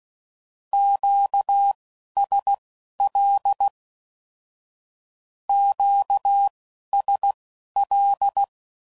Morse code, heard as a keyed tone, about 16 wpm, spaced as for 11 wpm, 800 Hz. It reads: QSL QSL
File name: CW Code Groups @10WPM / Q Calls / Q Calls Twice